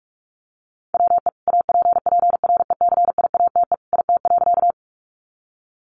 Morse code, heard as a keyed tone, about 44 wpm, 700 Hz. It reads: WE APPRECIATE IT.
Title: Sentences from Top 1300 Words 45wpm